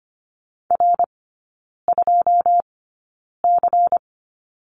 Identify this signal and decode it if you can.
Morse code, heard as a keyed tone, about 25 wpm, spaced as for 9 wpm, 700 Hz. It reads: R2C
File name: Lesson 17e - Letter u - Three Characters 25wpm